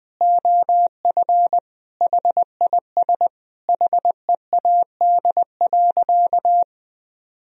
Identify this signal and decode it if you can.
Morse code, heard as a keyed tone, about 21 wpm, 700 Hz. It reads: OF HIS HEAD.
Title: Sentences from Top 1000 Words - Review Part 2 - 20wpm